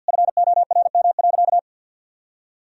Morse code, heard as a keyed tone, about 52 wpm, 700 Hz. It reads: FORM.
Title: Sentences from Top 1000 Words - Review Part 1 - 50wpm